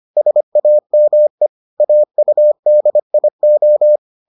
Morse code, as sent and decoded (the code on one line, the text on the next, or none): ... .- -- . / .- ..- -.. .. ---
SAME AUDIO